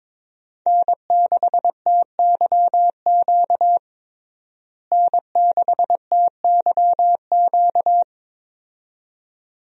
Morse code, heard as a keyed tone, about 22 wpm, 700 Hz. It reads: N6TYQ N6TYQ
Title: USA Call Signs - 5 Characters - Repeated 2x - 22wpm